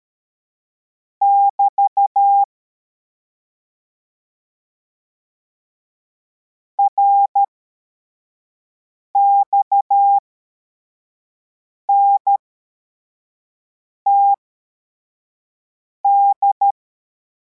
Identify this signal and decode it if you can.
Morse code, heard as a keyed tone, about 13 wpm, spaced as for 4 wpm, 800 Hz. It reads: <BT> RXNTD
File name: formation / cw / Text33